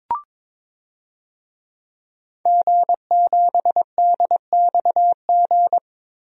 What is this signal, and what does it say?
Morse code, heard as a keyed tone, about 22 wpm, 700 Hz. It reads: G7DXG
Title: DXCC-100 Call Signs - 5 Characters - Repeated 3x - 22wpm